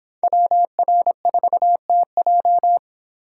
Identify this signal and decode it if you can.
Morse code, heard as a keyed tone, about 25 wpm, 700 Hz. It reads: WR4TJ